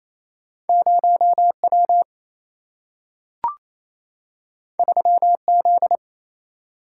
Morse code, heard as a keyed tone, about 28 wpm, 700 Hz. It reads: 0W 3Z
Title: Single-Letter Single-Number 28wpm